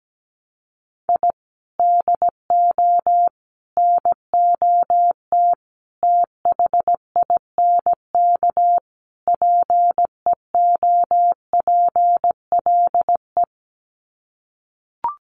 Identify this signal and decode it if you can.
Morse code, heard as a keyed tone, about 17 wpm, 700 Hz. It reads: I DO NOT THINK PEOPLE
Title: Sets of 5 Words Encore2 17wpm